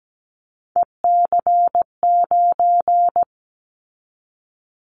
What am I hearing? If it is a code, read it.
Morse code, heard as a keyed tone, about 17 wpm, 700 Hz. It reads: EC9
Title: Callsign Prefixes for Top 100 DXCC - 17wpm